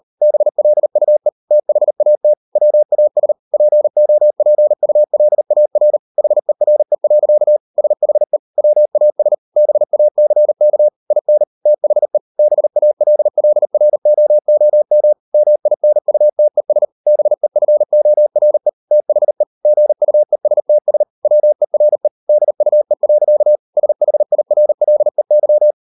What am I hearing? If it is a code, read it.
Morse code, heard as a keyed tone, about 40 wpm, 600 Hz. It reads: BLUE THAT WAS POPULAR HERE. SHE WAS BACK IN THE BALLROOM MINUTES BEFORE THE GUESTS WERE DUE. SHIRLEY